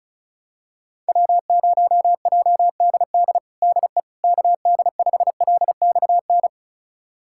Morse code, heard as a keyed tone, about 34 wpm, 700 Hz. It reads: W0JDD DE KD5LXN
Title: USA Callsign DE Callsign 35wpm